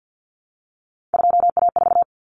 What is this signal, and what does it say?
Morse code, heard as a keyed tone, about 50 wpm, 700 Hz. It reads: 2A4
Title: Common Cut Numbers - 3-Digit 50wpm